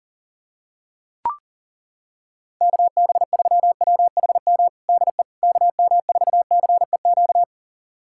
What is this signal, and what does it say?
Morse code, heard as a keyed tone, about 40 wpm, 700 Hz. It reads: KB3WHM DE KM4CEQ
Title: USA Callsign DE Callsign - 6 Characters - 40wpm